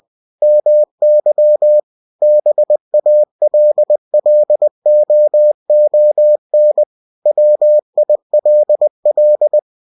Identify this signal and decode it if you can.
Morse code, heard as a keyed tone, about 20 wpm, 600 Hz. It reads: MY BALLOON WILL
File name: cw-books / winnie-the-pooh / winnie-20wpm-page-0057